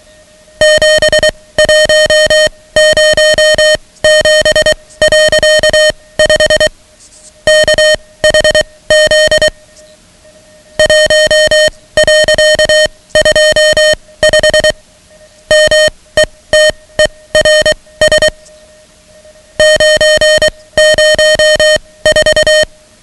Morse code, heard as a keyed tone, about 23 wpm, spaced as for 17 wpm, 620 Hz, with noise. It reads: 7107.5 KHZ 1.25 METERS 904
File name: ham radio / CodeQuick / Code Quick 10